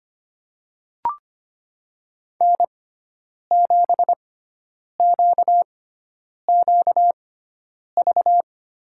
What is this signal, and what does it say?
Morse code, heard as a keyed tone, about 25 wpm, spaced as for 9 wpm, 700 Hz. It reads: N7QQV
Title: USA Call Signs - 5 Characters - 6x Character Spacing - 25wpm